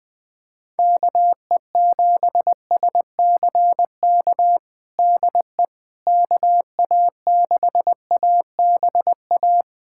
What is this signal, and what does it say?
Morse code, heard as a keyed tone, about 21 wpm, 700 Hz. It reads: KE7SCK DE KA6ABA
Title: USA Callsign DE Callsign - 6 Characters - 20wpm